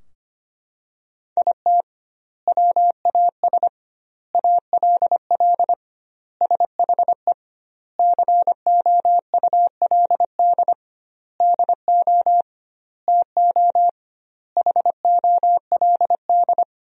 Morse code, heard as a keyed tone, about 25 wpm, 700 Hz. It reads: IT WAS ALL SHE COULD DO TO HOLD
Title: 2x Word-Spacing - Sentences from Top 400 Words - 25wpm